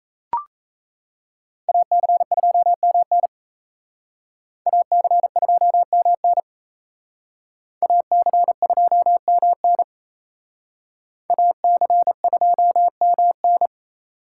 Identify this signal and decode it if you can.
Morse code, heard as a keyed tone, about 33 wpm, 700 Hz. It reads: AC2MN AC2MN AC2MN AC2MN
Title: Speed Racing - USA 2x2 Call Signs 28wpm